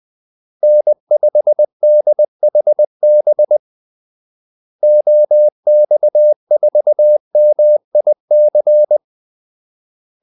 Morse code, heard as a keyed tone, about 20 wpm, 600 Hz. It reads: N5DHB OX4MIC
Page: Kaldesignaler 20 wpm